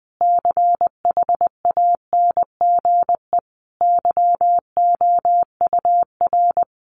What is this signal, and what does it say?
Morse code, heard as a keyed tone, about 20 wpm, 700 Hz. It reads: CHANGE YOUR
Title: Sentences from Top 1300 Words 20wpm